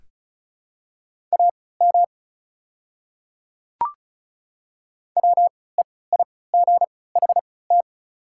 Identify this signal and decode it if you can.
Morse code, heard as a keyed tone, about 34 wpm, spaced as for 20 wpm, 700 Hz. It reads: AM WEIGHT
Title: 3x Character Spacing - Top 500 Words - 35wpm